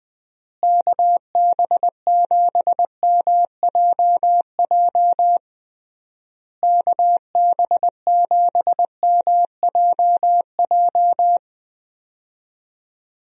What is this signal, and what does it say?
Morse code, heard as a keyed tone, about 20 wpm, 700 Hz. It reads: KB7MJJ KB7MJJ
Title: USA Call Signs - 6 Characters - Repeated 3x - 20wpm